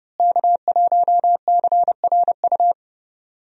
Morse code, heard as a keyed tone, about 30 wpm, 700 Hz. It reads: K1CRU
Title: USA Call Signs - 5 Characters - Repeated 3x - 30wpm